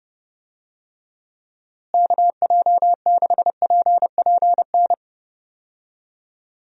Morse code, heard as a keyed tone, about 30 wpm, 700 Hz. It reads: KJ6PPN
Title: USA Call Signs - 6 Characters - Repeated 2x - 30wpm